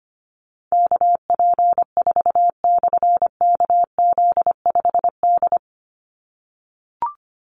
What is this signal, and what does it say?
Morse code, heard as a keyed tone, about 25 wpm, 700 Hz. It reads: KP4/KZ5D
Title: Call Signs - 25wpm